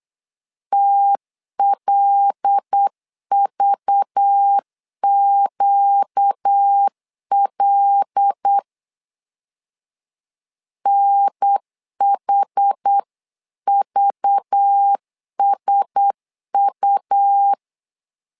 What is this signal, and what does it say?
Morse code, heard as a keyed tone, about 8 wpm, 800 Hz. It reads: TLVQL NHVSU
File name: LEZIONI / 5 LBVC / 5 LBV 8wpm 2837